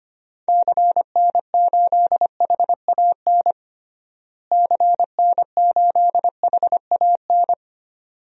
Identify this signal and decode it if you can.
Morse code, heard as a keyed tone, about 25 wpm, 700 Hz. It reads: CN8HAN CN8HAN
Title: DXCC-100 Call Signs - 6 Characters - Repeated 2x - 25wpm